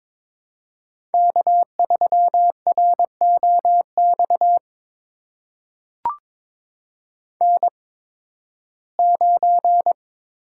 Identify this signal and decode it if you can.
Morse code, heard as a keyed tone, about 22 wpm, spaced as for 14 wpm, 700 Hz. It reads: K3ROX N9
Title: USA Call Signs - 5 Characters - 8x Character Spacing - 22wpm